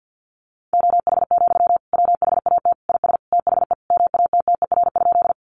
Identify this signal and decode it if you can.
Morse code, heard as a keyed tone, about 48 wpm, 700 Hz. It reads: OH, WHAT IS THE MATTER?
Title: Sentences from Top 700 Words 50wpm